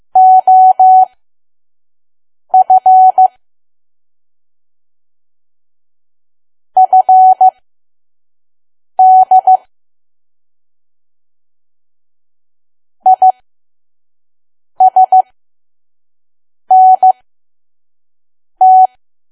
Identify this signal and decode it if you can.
Morse code, heard as a keyed tone, about 15 wpm, spaced as for 5 wpm, 750 Hz. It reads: OF FD ISNT
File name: Archive / 5 WPM files / PM